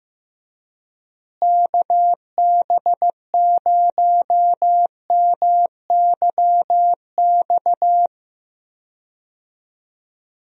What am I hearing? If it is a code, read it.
Morse code, heard as a keyed tone, about 15 wpm, 700 Hz. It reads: KB0MYX
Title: USA Call Signs - 6 Characters - Repeated 3x - 15wpm